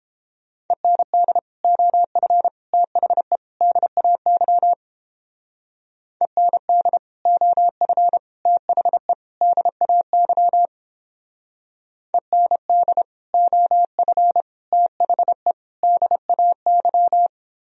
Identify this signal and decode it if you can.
Morse code, heard as a keyed tone, about 31 wpm, 700 Hz. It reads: END OF THE DAY END OF THE DAY END OF THE DAY